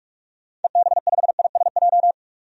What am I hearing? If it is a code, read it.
Morse code, heard as a keyed tone, about 42 wpm, 700 Hz. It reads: EB5ISJ
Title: DXCC-100 Call Signs - 6 Characters - 45wpm